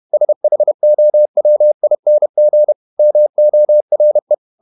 Morse code, heard as a keyed tone, about 31 wpm, 600 Hz. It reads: SHOWING MORE